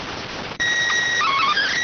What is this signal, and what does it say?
SSTV (slow-scan television), PD 290.